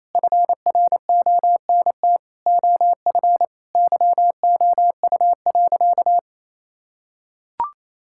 Morse code, heard as a keyed tone, about 28 wpm, 700 Hz. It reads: FRONT OF YOU.
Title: Sentences from Top 1000 Words - Review Part 6 - 28wpm